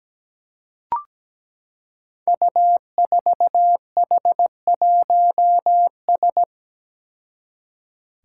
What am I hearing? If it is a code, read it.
Morse code, heard as a keyed tone, about 17 wpm, 700 Hz. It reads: U4H1S
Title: Five Challenging Letter-Number Combinations 17wpm